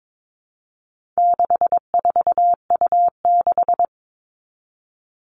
Morse code, heard as a keyed tone, about 22 wpm, 700 Hz. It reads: 64U6
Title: All Cut Numbers - 4-Digit 22wpm